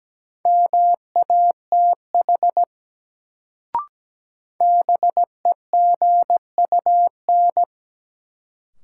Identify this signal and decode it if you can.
Morse code, heard as a keyed tone, about 17 wpm, 700 Hz. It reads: MATH BEGUN
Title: Sentences from Top 1000 Words - Review Part 6 - New Word Drill - 17wpm